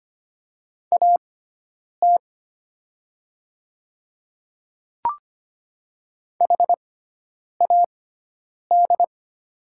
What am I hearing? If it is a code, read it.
Morse code, heard as a keyed tone, about 25 wpm, spaced as for 9 wpm, 700 Hz. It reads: AT HAD
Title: 6x Character Spacing - Top 300 Words - 25wpm